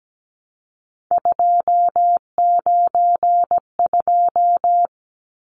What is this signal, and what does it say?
Morse code, heard as a keyed tone, about 17 wpm, 700 Hz. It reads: 292
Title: Numbers - 3-Digit 17wpm